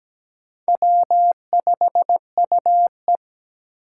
Morse code, heard as a keyed tone, about 20 wpm, 700 Hz. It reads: W5UE
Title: Speed Racing - USA 1x2 Call Signs 17wpm